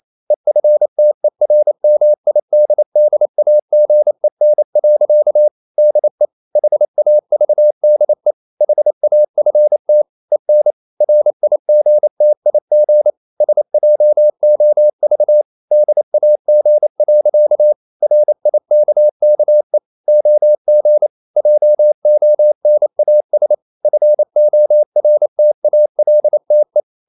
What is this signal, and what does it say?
Morse code, heard as a keyed tone, about 28 wpm, 600 Hz. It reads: EFTERMIDDAGEN. DE HAVDE HAFT EN RIGTIG SJOV DAG. RIKKE OG JONAS FORTALTE
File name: Ferie QRQ 28wpm